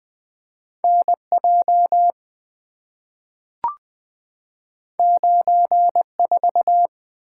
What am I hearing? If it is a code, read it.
Morse code, heard as a keyed tone, about 20 wpm, 700 Hz. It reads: N J 9 4
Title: Two Letter-Number Combinations - 20wpm